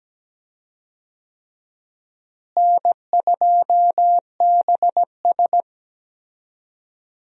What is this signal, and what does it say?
Morse code, heard as a keyed tone, about 17 wpm, 700 Hz. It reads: N2BS
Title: USA Call Signs - 4 Characters - Repeated 3x - 17wpm